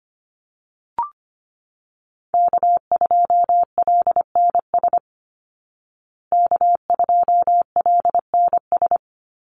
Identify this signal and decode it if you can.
Morse code, heard as a keyed tone, about 25 wpm, 700 Hz. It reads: K2LNS K2LNS
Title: Call Signs Repeated 3x - 25wpm